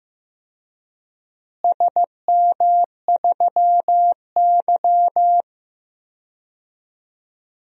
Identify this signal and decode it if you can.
Morse code, heard as a keyed tone, about 15 wpm, 700 Hz. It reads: SM3Y
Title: DXCC-100 Call Signs - 4 Characters - Repeated 2x - 15wpm